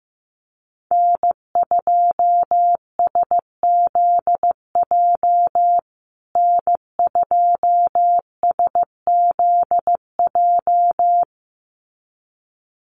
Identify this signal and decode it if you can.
Morse code, heard as a keyed tone, about 15 wpm, 700 Hz. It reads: N2SZJ N2SZJ
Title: USA Call Signs Repeated 2x 15wpm